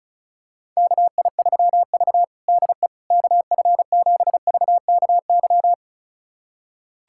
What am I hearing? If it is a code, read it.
Morse code, heard as a keyed tone, about 36 wpm, 700 Hz. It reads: KI3V DE KF7VKY
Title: USA Callsign DE Callsign 35wpm